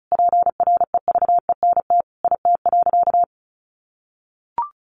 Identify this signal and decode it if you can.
Morse code, heard as a keyed tone, about 34 wpm, 700 Hz. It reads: PREVENT IT.
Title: Sentences from Top 1100 Words 35wpm